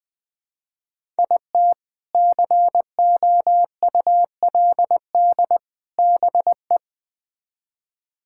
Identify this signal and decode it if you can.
Morse code, heard as a keyed tone, about 20 wpm, 700 Hz. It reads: IT COULD BE
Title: Build A Phrase - 3-Word Phrases - 20wpm